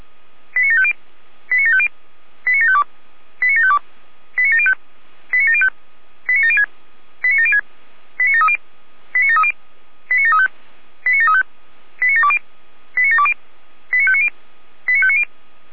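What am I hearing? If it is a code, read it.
Five-tone selective call (ZVEI), heard as quick runs of five tones, tones of about 70 ms. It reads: 89750 89750 89752 89752 89705 89705 89706 89706 89730 89730 89735 89735 89720 89720 89590 89590
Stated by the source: ZVEI five-tone selective call received by SDR.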